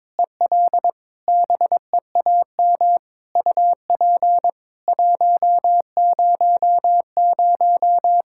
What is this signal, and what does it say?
Morse code, heard as a keyed tone, about 22 wpm, 700 Hz. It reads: EL BEAM UP 100
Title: K1USN SST QSO Practice 22wpm